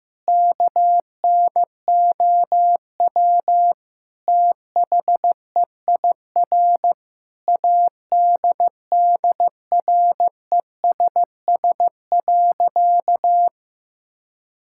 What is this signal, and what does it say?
Morse code, heard as a keyed tone, about 15 wpm, 700 Hz. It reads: KNOW THEIR ADDRESS.
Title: Sentences from Top 1000 Words - Review Part 6 - 15wpm